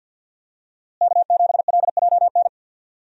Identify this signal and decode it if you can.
Morse code, heard as a keyed tone, about 53 wpm, 700 Hz. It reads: K7LJN
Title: USA Call Signs - 5 Characters - Repeated 2x - 50wpm